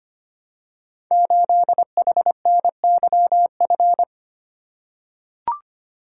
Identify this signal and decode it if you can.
Morse code, heard as a keyed tone, about 25 wpm, 700 Hz. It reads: 8HNYF
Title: Code Groups - 5 Letter:Numbers - 25wpm